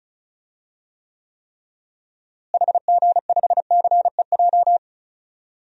Morse code, heard as a keyed tone, about 34 wpm, 700 Hz. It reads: HG5CEJ
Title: DXCC-100 Call Signs - 6 Characters - Repeated 3x - 35wpm